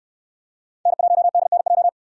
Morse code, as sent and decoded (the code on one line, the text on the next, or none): -. .---- -.. -. .---
N1DNJ